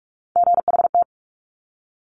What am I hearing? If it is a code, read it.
Morse code, heard as a keyed tone, about 46 wpm, 700 Hz. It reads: GHT
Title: Top Three Letter Combinations in Words 45wpm